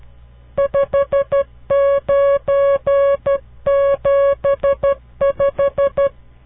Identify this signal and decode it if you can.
Morse code, heard as a keyed tone, about 12 wpm, 560 Hz. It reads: 5975
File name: CW Learning Programs / farnsworth / unit 5